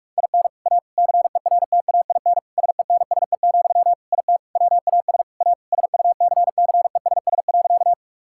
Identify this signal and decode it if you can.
Morse code, heard as a keyed tone, about 46 wpm, 700 Hz. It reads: IN A CERTAIN SENSE, IT WAS A SUCCESS.
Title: Sentences from Top 1000 Words - Review Part 6 - 45wpm